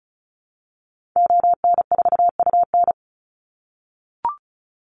Morse code, heard as a keyed tone, about 34 wpm, 700 Hz. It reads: ON4UN
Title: Call Signs - 35wpm